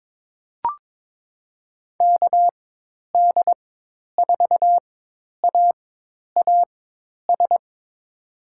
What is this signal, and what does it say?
Morse code, heard as a keyed tone, about 22 wpm, spaced as for 10 wpm, 700 Hz. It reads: KD4AAS
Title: USA Call Signs - 4x Character Spacing - 22wpm